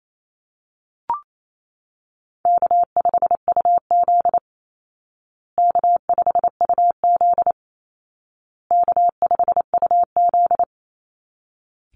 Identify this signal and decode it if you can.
Morse code, heard as a keyed tone, about 28 wpm, 700 Hz. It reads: K5UZ K5UZ K5UZ
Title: Call Signs Repeated 2x - 28wpm